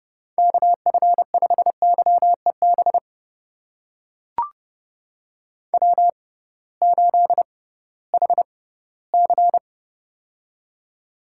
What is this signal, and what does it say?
Morse code, heard as a keyed tone, about 30 wpm, spaced as for 18 wpm, 700 Hz. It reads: KF5YEB W8HC
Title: USA Call Signs - 6x Character Spacing - 30wpm